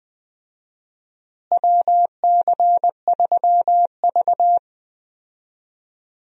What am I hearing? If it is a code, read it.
Morse code, heard as a keyed tone, about 20 wpm, 690 Hz. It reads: WC3V
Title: USA Call Signs - 4 Characters - Repeated 3x - 20wpm